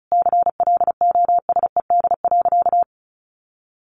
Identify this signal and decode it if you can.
Morse code, heard as a keyed tone, about 34 wpm, 700 Hz. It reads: CLOSED.
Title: Sentences from Top 1400 Words 35wpm